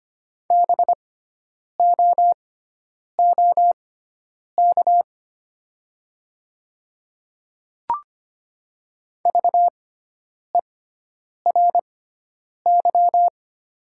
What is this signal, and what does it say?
Morse code, heard as a keyed tone, about 25 wpm, spaced as for 9 wpm, 700 Hz. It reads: BOOK VERY
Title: 6x Character Spacing - Top 300 Words - 25wpm